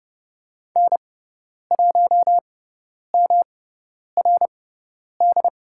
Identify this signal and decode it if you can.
Morse code, heard as a keyed tone, about 30 wpm, spaced as for 10 wpm, 700 Hz. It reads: N1MRD
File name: Lesson 18g - Letter m - Call Signs 30wpm